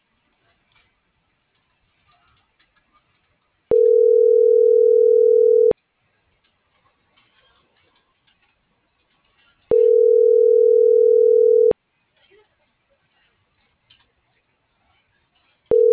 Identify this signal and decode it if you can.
Ringback tone (North America), 440 + 480 Hz.